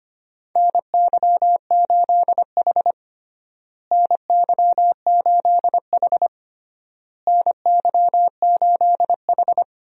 Morse code, heard as a keyed tone, about 25 wpm, 700 Hz. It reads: NY8H NY8H NY8H
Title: USA Call Signs - 4 Characters - Repeated 3x - 25wpm